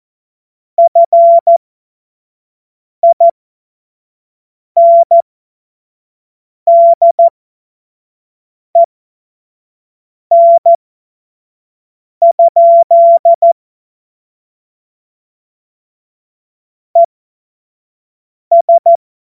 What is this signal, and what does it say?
Morse code, heard as a keyed tone, about 14 wpm, spaced as for 5 wpm, 680 Hz. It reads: FINDEN? ES
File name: Morsekurs / Witze / Witz 02. August 2022 25 F70